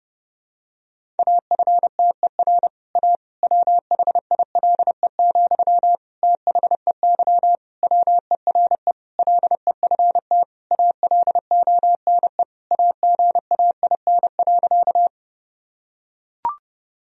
Morse code, heard as a keyed tone, about 32 wpm, 700 Hz. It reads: AFTER A WHILE, THEY WERE LEFT ALONE AGAIN.
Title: Sentences from Top 1000 Words - Review Part 1 - 30wpm